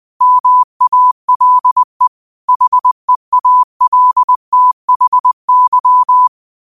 Morse code, heard as a keyed tone, about 20 wpm, 1000 Hz. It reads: MALE HEALTHY
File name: 20wpm